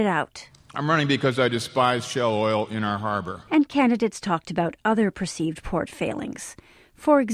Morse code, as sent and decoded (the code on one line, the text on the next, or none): none